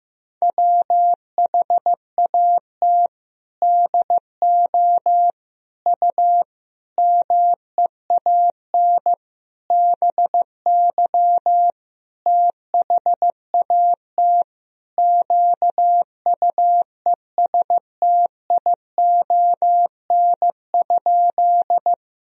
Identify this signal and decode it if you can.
Morse code, heard as a keyed tone, about 15 wpm, 700 Hz. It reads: WHAT DO U MEAN BY THAT QUESTION?
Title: Sentences from Top 400 Words and Abbreviations 15wpm